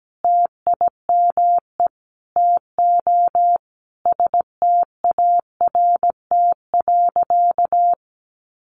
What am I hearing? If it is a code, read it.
Morse code, heard as a keyed tone, about 17 wpm, 700 Hz. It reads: TIME TO START.
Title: Sentences from Top 300 Words 17wpm